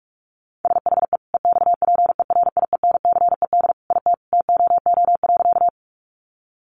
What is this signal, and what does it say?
Morse code, heard as a keyed tone, about 44 wpm, 700 Hz. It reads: SHE EXPERIENCED IT TOO.
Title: Sentences from Top 1100 Words 45wpm